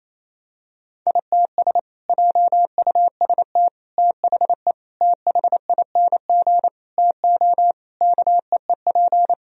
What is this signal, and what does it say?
Morse code, heard as a keyed tone, about 28 wpm, 700 Hz. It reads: ITS JUST THE THING TO KEEP
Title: Sentences from Top 1000 Words - Review Part 2 - 28wpm